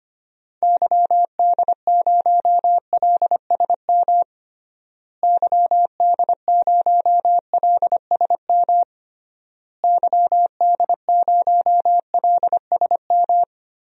Morse code, heard as a keyed tone, about 25 wpm, 700 Hz. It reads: YD0LSM YD0LSM YD0LSM
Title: DXCC-100 Call Signs - 6 Characters - Repeated 3x - 25wpm